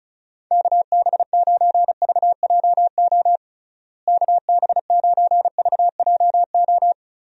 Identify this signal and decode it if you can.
Morse code, heard as a keyed tone, about 34 wpm, 700 Hz. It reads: KB9VJO KB9VJO
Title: USA Call Signs - 6 Characters - Repeated 3x - 35wpm